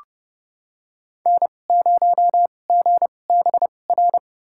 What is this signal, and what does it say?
Morse code, heard as a keyed tone, about 30 wpm, spaced as for 22 wpm, 690 Hz. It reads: N0GBR
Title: USA Call Signs - 5 Characters - 2x Character Spacing - 30wpm